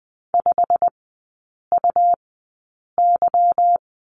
Morse code, heard as a keyed tone, about 20 wpm, spaced as for 8 wpm, 700 Hz. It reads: 5UY
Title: 159 Lesson 26e - Letter g - Three Characters 20wpm